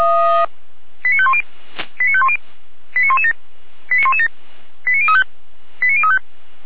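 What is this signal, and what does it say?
Five-tone selective call (ZVEI), heard as quick runs of five tones, tones of about 80 ms. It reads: 89510 89510 89117 89117 89035 89035
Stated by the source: ZVEI five-tone selective call received on SDR, from feuerwehr-orlamuende (fire brigade Orlamünde).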